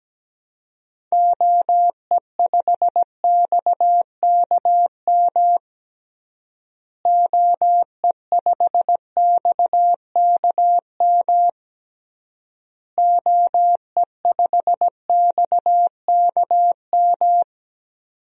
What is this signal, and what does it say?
Morse code, heard as a keyed tone, about 17 wpm, 700 Hz. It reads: OE5XKM OE5XKM OE5XKM
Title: DXCC-100 Call Signs - 6 Characters - Repeated 3x - 17wpm